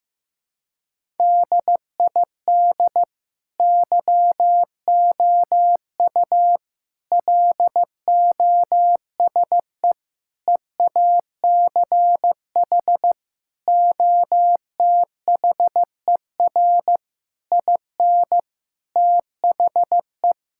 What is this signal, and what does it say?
Morse code, heard as a keyed tone, about 15 wpm, 700 Hz. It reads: DID YOU LOSE EACH OTHER IN THE
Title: Sentences from Top 1600 Words 15wpm